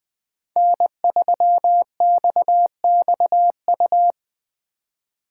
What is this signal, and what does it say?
Morse code, heard as a keyed tone, about 20 wpm, 700 Hz. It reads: N3XXU